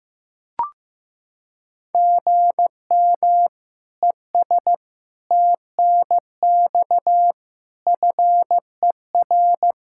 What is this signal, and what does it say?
Morse code, heard as a keyed tone, about 15 wpm, 700 Hz. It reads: GM ES TNX FER
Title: Practice QSO - Exchange 1 - Beginning - 15wpm